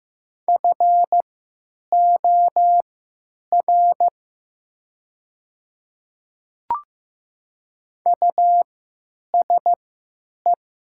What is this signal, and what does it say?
Morse code, heard as a keyed tone, about 15 wpm, spaced as for 9 wpm, 690 Hz. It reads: FOR USE
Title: 3x Character Spacing - Top 100 Words - 15wpm